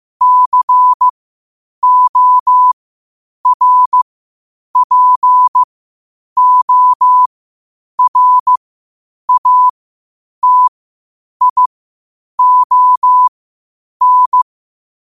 Morse code, heard as a keyed tone, about 15 wpm, spaced as for 8 wpm, 1000 Hz. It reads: CORPORATION
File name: qotdfiles / 8wpm